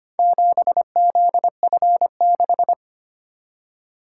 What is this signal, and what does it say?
Morse code, heard as a keyed tone, about 25 wpm, 700 Hz. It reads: 7ZF6
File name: practice / 25wpm / Four Letter-Number Combinations - 25wpm